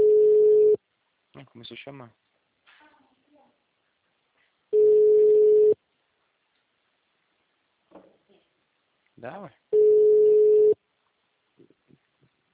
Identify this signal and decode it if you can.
Ringback tone (Europe, 425 Hz), with speech.
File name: out- -2039- - - .2